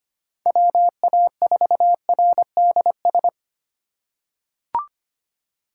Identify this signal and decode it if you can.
Morse code, heard as a keyed tone, about 25 wpm, 700 Hz. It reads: WA4RDS
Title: USA Call Signs - 6 Characters - 2x Character Spacing - 25wpm